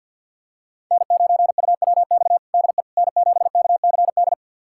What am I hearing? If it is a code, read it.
Morse code, heard as a keyed tone, about 53 wpm, 700 Hz. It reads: N9UWX DE N7KCD
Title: USA Callsign DE Callsign - 5 Characters - 50wpm